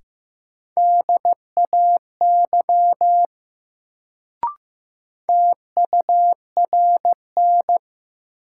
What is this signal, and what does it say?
Morse code, heard as a keyed tone, about 15 wpm, 700 Hz. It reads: DAY TURN
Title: Sentences from Top 200 Words - New Word Drill - 15wpm